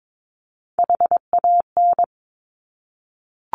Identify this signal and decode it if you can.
Morse code, heard as a keyed tone, about 22 wpm, 700 Hz. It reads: HAN
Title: Top Three Letter Combinations in Words 22wpm